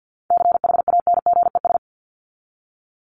Morse code, heard as a keyed tone, about 50 wpm, 700 Hz. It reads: CHANGES
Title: ES:S Suffix - Top 100 Words with Base Words 50wpm